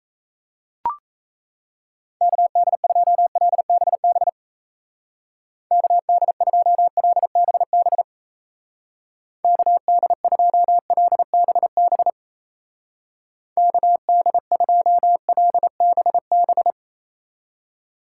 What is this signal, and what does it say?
Morse code, heard as a keyed tone, about 35 wpm, 700 Hz. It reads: KD2LBB KD2LBB KD2LBB KD2LBB
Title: Speed Racing - USA 2x3 Call Signs 28wpm